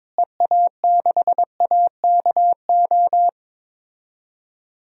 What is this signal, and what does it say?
Morse code, heard as a keyed tone, about 22 wpm, 700 Hz. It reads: EA6AKO
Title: DXCC-100 Call Signs - 6 Characters - Repeated 3x - 22wpm